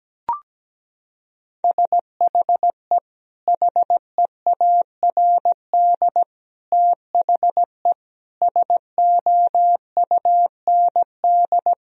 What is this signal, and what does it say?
Morse code, heard as a keyed tone, about 17 wpm, 700 Hz. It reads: SHE HEARD THE SOUND
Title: Sentences from Top 1000 Words - Review Part 1 - 17wpm